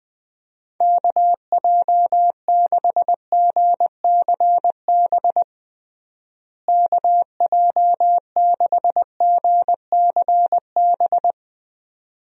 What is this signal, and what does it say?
Morse code, heard as a keyed tone, about 21 wpm, 700 Hz. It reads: KJ6GCB KJ6GCB
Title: USA Call Signs - 6 Characters - Repeated 2x - 20wpm